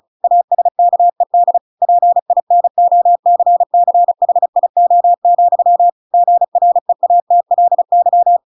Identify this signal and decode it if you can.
Morse code, heard as a keyed tone, about 34 wpm, 700 Hz. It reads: ASKED PINOCCHIO, GREATLY